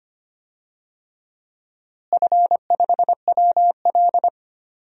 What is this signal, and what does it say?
Morse code, heard as a keyed tone, about 25 wpm, 700 Hz. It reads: F5WL
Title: DXCC-100 Call Signs - 4 Characters - Repeated 3x - 25wpm